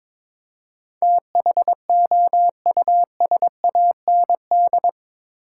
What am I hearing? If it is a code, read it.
Morse code, heard as a keyed tone, about 22 wpm, 700 Hz. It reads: THOUSAND
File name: practice / 22wpm / Eight Letter Words - 22wpm